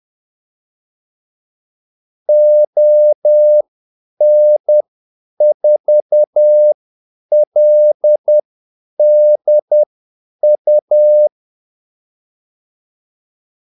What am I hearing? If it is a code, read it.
Morse code, heard as a keyed tone, about 10 wpm, spaced as for 8 wpm, 600 Hz. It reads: ON4LDU
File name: Callsigns-10-8wpm